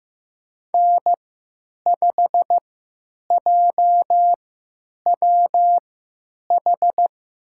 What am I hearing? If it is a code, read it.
Morse code, heard as a keyed tone, about 15 wpm, spaced as for 9 wpm, 700 Hz. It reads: N5JWH
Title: USA Call Signs - 5 Characters - 3x Character Spacing - 15wpm